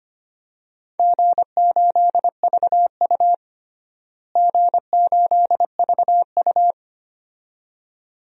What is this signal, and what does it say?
Morse code, heard as a keyed tone, about 25 wpm, 700 Hz. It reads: G8VU G8VU
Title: DXCC-100 Call Signs - 4 Characters - Repeated 2x - 25wpm